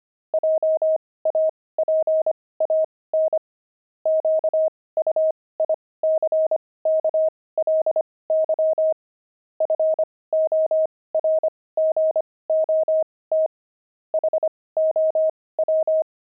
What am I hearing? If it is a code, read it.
Morse code, heard as a keyed tone, about 25 wpm, spaced as for 18 wpm, 630 Hz. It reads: JAPAN QUICKLY FORGOT HOW
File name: pangram-fs-18-cs-25-devils